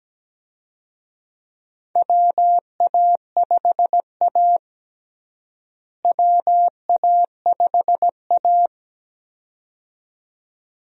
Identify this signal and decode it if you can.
Morse code, heard as a keyed tone, about 17 wpm, 700 Hz. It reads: WA5A WA5A
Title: USA Call Signs - 4 Characters - Repeated 2x - 17wpm